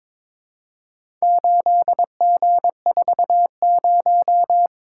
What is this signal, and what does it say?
Morse code, heard as a keyed tone, about 22 wpm, 700 Hz. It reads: 8G40
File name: practice / 22wpm / Four Letter-Number Combinations - 22wpm